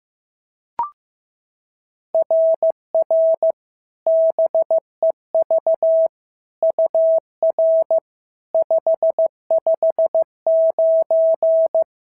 Morse code, heard as a keyed tone, about 14 wpm, 650 Hz. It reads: RR BEV UR 559
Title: SOTA QSO Practice 15wpm